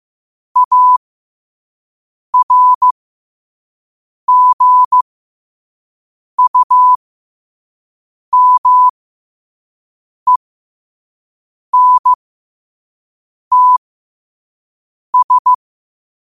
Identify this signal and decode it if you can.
Morse code, heard as a keyed tone, about 15 wpm, spaced as for 5 wpm, 1000 Hz. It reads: ARGUMENTS
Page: qotd-at-5-wpm